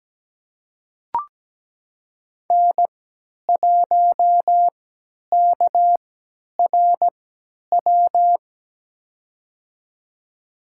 Morse code, heard as a keyed tone, about 17 wpm, spaced as for 10 wpm, 700 Hz. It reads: N1KRW
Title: USA Call Signs - 3x Character Spacing - 17wpm